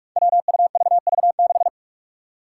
Morse code, heard as a keyed tone, about 44 wpm, 700 Hz. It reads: WUVV6
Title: Five Challenging Letter-Number Combinations 45wpm